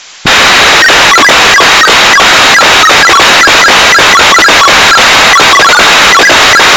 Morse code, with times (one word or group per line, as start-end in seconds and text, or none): none